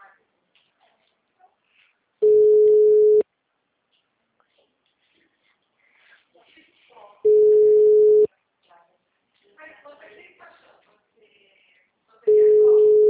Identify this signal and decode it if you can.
Ringback tone (Europe, 425 Hz), with speech.